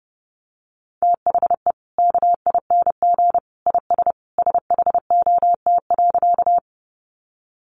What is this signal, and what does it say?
Morse code, heard as a keyed tone, about 30 wpm, 700 Hz. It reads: THE KING IS SHOT.
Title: Sentences from Top 1100 Words 30wpm